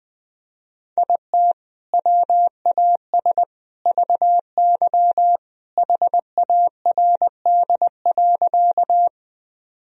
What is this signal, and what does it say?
Morse code, heard as a keyed tone, about 20 wpm, 700 Hz. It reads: IT WAS VY HARD.
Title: Sentences from Top 300 Words and Abbreviations 20wpm